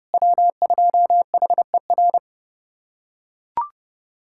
Morse code, heard as a keyed tone, about 30 wpm, 700 Hz. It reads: W2HER
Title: USA Callsign DE Callsign - 5 Characters - 30wpm